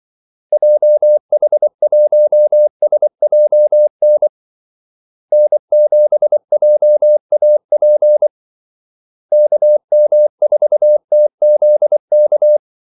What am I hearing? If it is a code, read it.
Morse code, heard as a keyed tone, about 24 wpm, 600 Hz. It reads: JH1SJN N7JAP KM4TZK
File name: Callsigns-24wpm